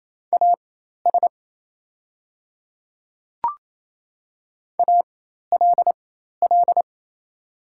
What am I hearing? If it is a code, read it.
Morse code, heard as a keyed tone, about 27 wpm, spaced as for 13 wpm, 700 Hz. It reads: AS ALL
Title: 4x Character Spacing - Top 100 Words - 28wpm